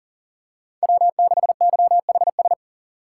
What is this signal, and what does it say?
Morse code, heard as a keyed tone, about 37 wpm, 700 Hz. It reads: W6YHS
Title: USA Call Signs - 5 Characters - 4x Character Spacing - 40wpm